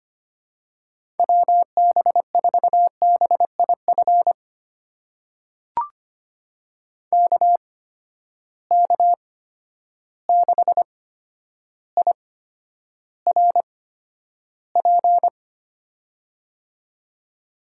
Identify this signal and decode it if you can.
Morse code, heard as a keyed tone, about 25 wpm, spaced as for 11 wpm, 700 Hz. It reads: WB4BIF KK6IRP
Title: USA Call Signs - 6 Characters - 8x Character Spacing - 25wpm